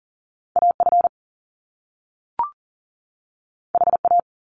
Morse code, heard as a keyed tone, about 41 wpm, 700 Hz. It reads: AF HA
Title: Top Two Letter Combinations in Words 40wpm